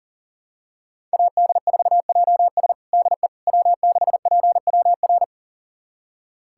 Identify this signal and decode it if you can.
Morse code, heard as a keyed tone, about 40 wpm, 690 Hz. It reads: AD4JS DE W6PWR
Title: USA Callsign DE Callsign - 5 Characters - 40wpm